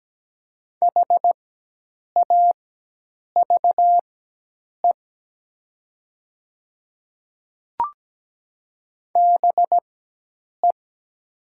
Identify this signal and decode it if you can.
Morse code, heard as a keyed tone, about 17 wpm, spaced as for 8 wpm, 700 Hz. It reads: HAVE BE
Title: 4x Character Spacing - Top 100 Words - 17wpm